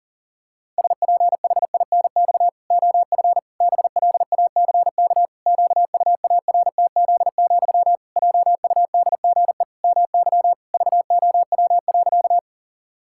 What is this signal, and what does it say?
Morse code, heard as a keyed tone, about 40 wpm, 700 Hz. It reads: SPHINX OF BLACK QUARTZ, JUDGE MY VOW.
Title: Pangram Challenge v2 - 40wpm